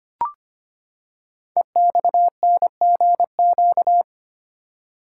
Morse code, heard as a keyed tone, about 25 wpm, 700 Hz. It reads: EXNGQ
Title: Code Groups - 5 Letters - 25wpm